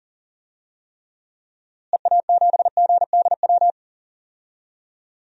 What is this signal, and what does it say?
Morse code, heard as a keyed tone, about 40 wpm, 700 Hz. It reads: EA7GDW
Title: DXCC-100 Call Signs - 6 Characters - Repeated 2x - 40wpm